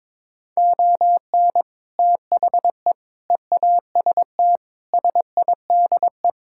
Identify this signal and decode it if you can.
Morse code, heard as a keyed tone, about 21 wpm, 700 Hz. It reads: ON THE EAST SIDE
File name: practice / 22wpm / Sets of 6 Words - 22wpm